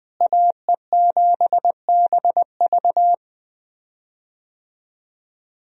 Morse code, heard as a keyed tone, about 20 wpm, 700 Hz. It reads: AE7BV
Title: USA Call Signs - 5 Characters - Repeated 3x - 20wpm